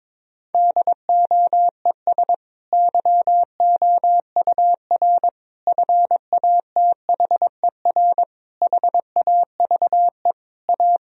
Morse code, heard as a keyed tone, about 22 wpm, 700 Hz. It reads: DOES YOUR FATHER HAVE A
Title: Sentences from Top 1000 Words - Review Part 3 - 22wpm